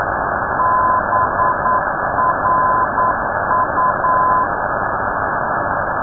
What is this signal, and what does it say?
Morse code, heard as a keyed tone, about 9 wpm, 1020 Hz. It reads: BRU
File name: europe / d-3079-cw-id-2005-rb